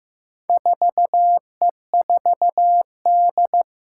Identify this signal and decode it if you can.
Morse code, heard as a keyed tone, about 15 wpm, 700 Hz. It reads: 4E4D
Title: Four Challenging Letter-Number Combinations 15wpm